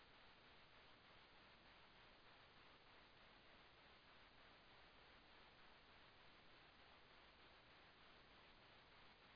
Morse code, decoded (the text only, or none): none